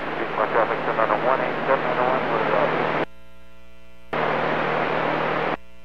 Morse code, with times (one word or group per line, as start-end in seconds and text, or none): none